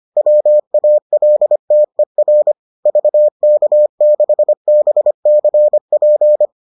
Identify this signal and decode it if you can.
Morse code, heard as a keyed tone, about 25 wpm, 600 Hz. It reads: WALTER VK6BCP